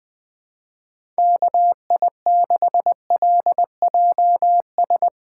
Morse code, heard as a keyed tone, about 20 wpm, 700 Hz. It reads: KI6LJS
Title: Speed Racing - USA 2x3 Call Signs 17wpm